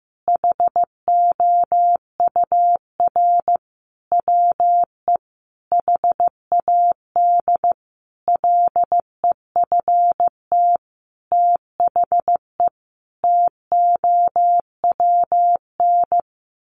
Morse code, heard as a keyed tone, about 15 wpm, 700 Hz. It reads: HOUR WE HAD LEFT THE TOWN
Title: Sentences from Top 1300 Words 15wpm